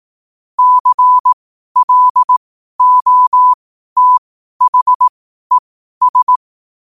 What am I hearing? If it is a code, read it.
Morse code, heard as a keyed tone, about 18 wpm, spaced as for 12 wpm, 1000 Hz. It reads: CLOTHES
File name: qotdfiles / 12wpm